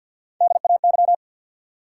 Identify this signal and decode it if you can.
Morse code, heard as a keyed tone, about 49 wpm, 700 Hz. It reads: DAY
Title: Sets of 2 Words - New Word Drill - 50wpm